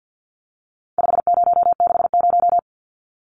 Morse code, heard as a keyed tone, about 50 wpm, 700 Hz. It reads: 5060
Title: Numbers - 4-Digit 50wpm